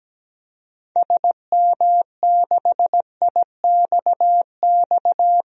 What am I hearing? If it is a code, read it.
Morse code, heard as a keyed tone, about 17 wpm, 700 Hz. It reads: SM6IXX
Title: DXCC-100 Call Signs - 6 Characters - 17wpm